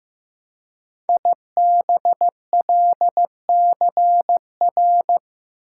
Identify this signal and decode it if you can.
Morse code, heard as a keyed tone, about 15 wpm, 700 Hz. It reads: IBLCR
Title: Code Groups - 5 Letter:Numbers - 15wpm